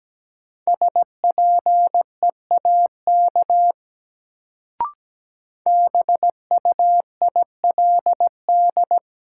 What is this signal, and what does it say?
Morse code, heard as a keyed tone, about 17 wpm, 700 Hz. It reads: SPEAK BUILD
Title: Sentences from Top 1100 Words - New Word Drill Alt - 17wpm